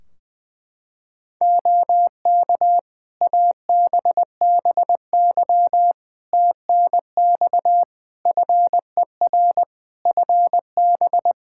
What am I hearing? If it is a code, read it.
Morse code, heard as a keyed tone, about 20 wpm, 700 Hz. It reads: OK ABBY TNX FER FB
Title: Practice QSO - Exchange 4 - Ending - 20wpm